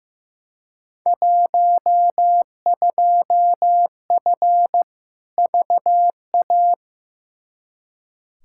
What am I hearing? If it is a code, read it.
Morse code, heard as a keyed tone, about 15 wpm, 700 Hz. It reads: 12F VA
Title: Field Day Exchanges Encore - 15wpm